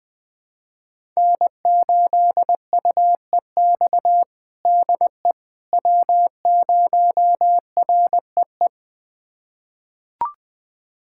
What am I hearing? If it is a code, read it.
Morse code, heard as a keyed tone, about 21 wpm, 700 Hz. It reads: N8UEX DE W0REE
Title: USA Callsign DE Callsign - 5 Characters - 20wpm